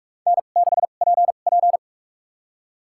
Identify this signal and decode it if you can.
Morse code, heard as a keyed tone, about 46 wpm, spaced as for 32 wpm, 700 Hz. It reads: N6PP
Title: USA Call Signs - 4 Characters - 2x Character Spacing - 45wpm